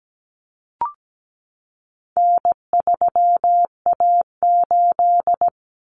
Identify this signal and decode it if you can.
Morse code, heard as a keyed tone, about 17 wpm, 700 Hz. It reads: N3A8
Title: Common Cut Numbers - 4-Digit 17wpm